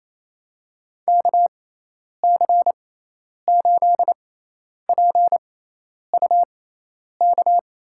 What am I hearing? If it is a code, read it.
Morse code, heard as a keyed tone, about 27 wpm, spaced as for 10 wpm, 700 Hz. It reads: KC8PUK
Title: USA Call Signs - 6x Character Spacing - 28wpm